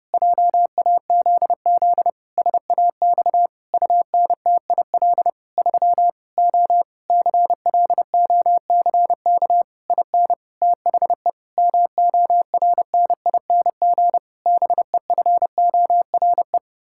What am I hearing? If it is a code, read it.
Morse code, heard as a keyed tone, about 29 wpm, 700 Hz. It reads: JAZZ SAX UNTIL 3 O CLOCK IN THE MORNING BEFORE
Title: Pangram Challenge v2 - 30wpm